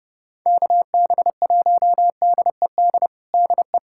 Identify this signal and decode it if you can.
Morse code, heard as a keyed tone, about 32 wpm, 700 Hz. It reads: KB1DED DE
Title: USA Callsign DE Callsign - 6 Characters - 30wpm